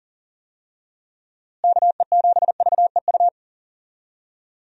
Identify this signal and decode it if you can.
Morse code, heard as a keyed tone, about 40 wpm, 700 Hz. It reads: KE7VEU
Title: USA Call Signs - 6 Characters - Repeated 2x - 40wpm